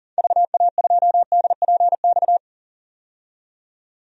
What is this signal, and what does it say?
Morse code, heard as a keyed tone, about 40 wpm, 700 Hz. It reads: VA2DPX
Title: DXCC-100 Call Signs - 6 Characters - 40wpm